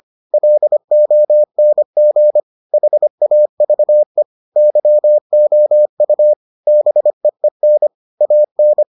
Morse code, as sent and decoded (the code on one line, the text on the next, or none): .-.. --- -. --. / .... .- ...- . / -.-- --- ..- / -... . -. / .- -.
LONG HAVE YOU BEEN AN